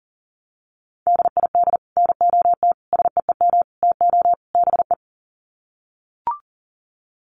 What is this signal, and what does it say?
Morse code, heard as a keyed tone, about 40 wpm, 700 Hz. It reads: DID NOT SEEM TO BE
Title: Sets of 5 Words Encore 40wpm